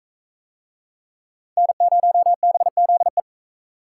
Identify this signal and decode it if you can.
Morse code, heard as a keyed tone, about 40 wpm, 700 Hz. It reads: N0BZE